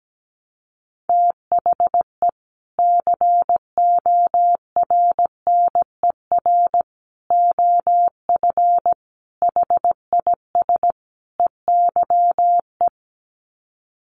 Morse code, heard as a keyed tone, about 17 wpm, 700 Hz. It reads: THE CORNER OF HIS EYE
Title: Sets of 5 Words Encore2 17wpm